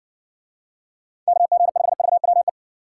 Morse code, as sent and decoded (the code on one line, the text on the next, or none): -.- --. ..... ...- .-- .
KG5VWE